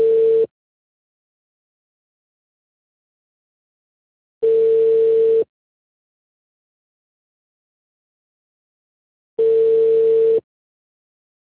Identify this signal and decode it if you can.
Ringback tone (China, 450 Hz).